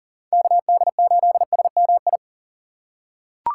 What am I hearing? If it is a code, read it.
Morse code, heard as a keyed tone, about 40 wpm, 700 Hz. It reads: KD8SMI
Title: USA Call Signs - 6 Characters - 4x Character Spacing - 40wpm